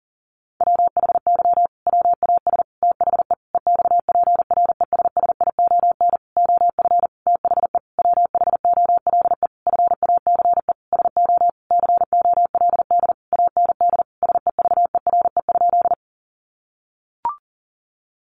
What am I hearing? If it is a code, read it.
Morse code, heard as a keyed tone, about 40 wpm, 700 Hz. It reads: WHY WAS THE EXPRESSION OF THE WHOLE FACE SO COLD AND SEVERE?
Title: Sentences from Top 1400 Words 40wpm